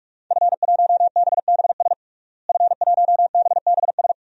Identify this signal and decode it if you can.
Morse code, heard as a keyed tone, about 46 wpm, 700 Hz. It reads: F1BBS F1BBS
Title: DXCC-100 Call Signs - 5 Characters - Repeated 3x - 45wpm